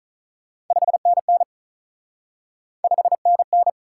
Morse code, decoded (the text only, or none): 5NN 5NN